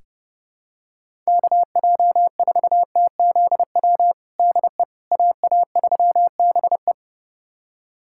Morse code, heard as a keyed tone, about 32 wpm, 700 Hz. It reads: KJ4TZW DE AA3BE
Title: USA Callsign DE Callsign 30wpm